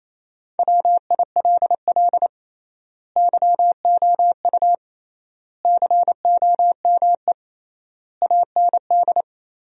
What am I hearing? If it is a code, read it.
Morse code, heard as a keyed tone, about 28 wpm, 700 Hz. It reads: WILL YOU COME AND
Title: 3x Word-Spacing - Sentences from Top 100 Words - 28wpm